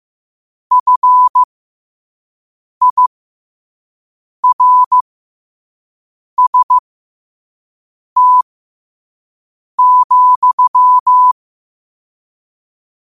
Morse code, heard as a keyed tone, about 15 wpm, spaced as for 5 wpm, 1000 Hz. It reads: FIRST,
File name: qotdfiles / 5wpm